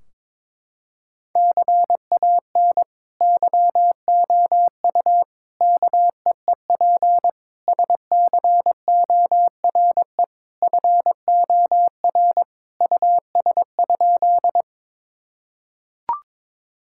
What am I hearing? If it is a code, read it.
Morse code, heard as a keyed tone, about 22 wpm, 700 Hz. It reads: CAN YOU KEEP SCORE FOR US?
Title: Sentences from Top 1600 Words 22wpm